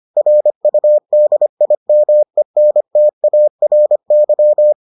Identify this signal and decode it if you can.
Morse code, heard as a keyed tone, about 25 wpm, 600 Hz. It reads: RUDIMENTARY